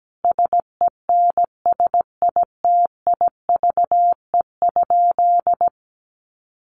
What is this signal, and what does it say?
Morse code, heard as a keyed tone, about 17 wpm, 700 Hz. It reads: SENSITIVE?
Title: Sentences from Top 1400 Words 17wpm